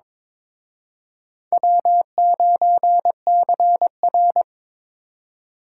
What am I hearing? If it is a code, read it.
Morse code, heard as a keyed tone, about 26 wpm, 700 Hz. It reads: W9CR
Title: Speed Racing - USA 1x2 Call Signs 22wpm